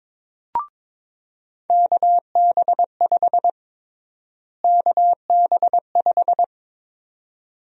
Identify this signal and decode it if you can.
Morse code, heard as a keyed tone, about 21 wpm, 700 Hz. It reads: KB5 KB5
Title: USA Call Sign Prefixes - Repeated 2x - 22wpm